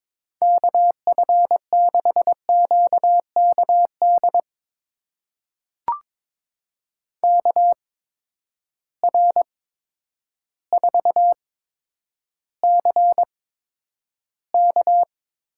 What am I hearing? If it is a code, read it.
Morse code, heard as a keyed tone, about 22 wpm, spaced as for 10 wpm, 700 Hz. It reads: KF6QKD KR4CK
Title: USA Call Signs - 6 Characters - 8x Character Spacing - 22wpm